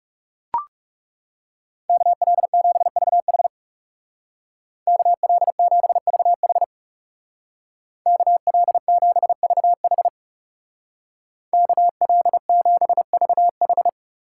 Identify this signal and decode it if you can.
Morse code, heard as a keyed tone, about 36 wpm, 700 Hz. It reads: KL7VH KL7VH KL7VH KL7VH
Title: Speed Racing - USA 2x2 Call Signs 30wpm